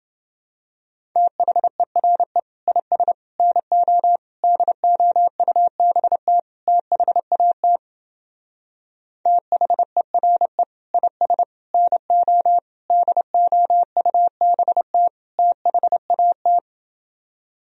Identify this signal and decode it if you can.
Morse code, heard as a keyed tone, about 28 wpm, 700 Hz. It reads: THERE IS NO DOUBT THAT THERE IS NO DOUBT THAT